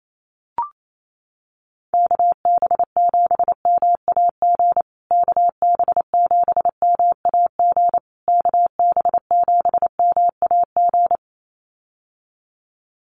Morse code, heard as a keyed tone, about 28 wpm, 700 Hz. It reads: KB7MAG KB7MAG KB7MAG
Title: USA Call Signs Repeated 3x 28wpm